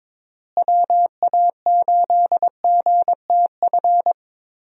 Morse code, heard as a keyed tone, about 21 wpm, 700 Hz. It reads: WA8GTF
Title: USA Call Signs - 6 Characters - Repeated 3x - 22wpm